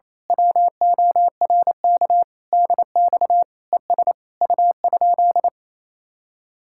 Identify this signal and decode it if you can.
Morse code, heard as a keyed tone, about 27 wpm, 700 Hz. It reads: WORK DX ES U?
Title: Speed-Racing Common QSO Phrases - 28wpm